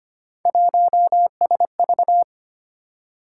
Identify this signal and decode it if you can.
Morse code, heard as a keyed tone, about 25 wpm, 700 Hz. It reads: 1SV
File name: Three Letter-Number Combinations - 25wpm